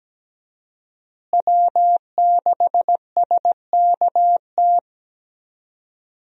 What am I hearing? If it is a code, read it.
Morse code, heard as a keyed tone, about 17 wpm, 700 Hz. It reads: W6SKT
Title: USA Call Signs - 5 Characters - Repeated 2x - 17wpm